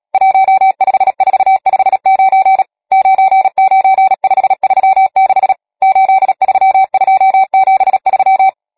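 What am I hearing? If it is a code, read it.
Morse code, heard as a keyed tone, about 37 wpm, 770 Hz. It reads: 15459 99536 83273
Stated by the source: Morse code number practice, machine-sent training text (Nácvik telegrafie, OK1FCB).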